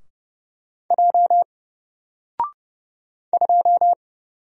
Morse code, heard as a keyed tone, about 30 wpm, 700 Hz. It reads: J 2
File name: Single-Letter-Number-30wpm